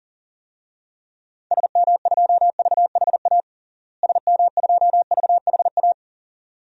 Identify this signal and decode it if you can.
Morse code, heard as a keyed tone, about 40 wpm, 700 Hz. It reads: SM2VHA SM2VHA
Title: DXCC-100 Call Signs - 6 Characters - Repeated 2x - 40wpm